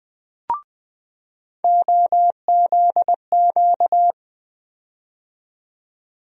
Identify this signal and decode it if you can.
Morse code, heard as a keyed tone, about 20 wpm, 700 Hz. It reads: OZQ
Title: USA Call Sign Postfixes - 3 Letters 20wpm